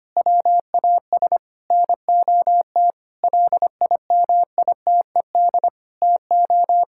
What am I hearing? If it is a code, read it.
Morse code, heard as a keyed tone, about 25 wpm, 700 Hz. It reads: WAS NOT LIMITED TO
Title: Sentences from Top 1000 Words - Review Part 6 - 25wpm